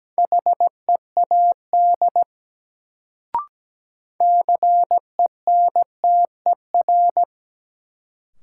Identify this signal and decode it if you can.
Morse code, heard as a keyed tone, about 17 wpm, 700 Hz. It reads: HEAD CENTER
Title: Sets of 3 Words - New Word Drill - 17wpm